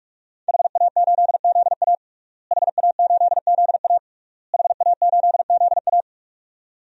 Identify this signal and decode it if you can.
Morse code, heard as a keyed tone, about 46 wpm, 700 Hz. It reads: HA8ZA HA8ZA HA8ZA
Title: DXCC-100 Call Signs - 5 Characters - Repeated 3x - 45wpm